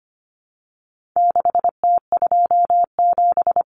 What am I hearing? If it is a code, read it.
Morse code, heard as a keyed tone, about 25 wpm, 700 Hz. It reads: 6T27
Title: Common Cut Numbers - 4-Digit 25wpm